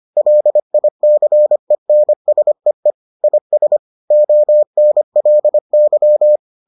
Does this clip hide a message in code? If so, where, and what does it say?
Yes: Morse code from 0.2 to 6.4 s, heard as a keyed tone, about 25 wpm, 600 Hz. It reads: LICENSEE IS ONLY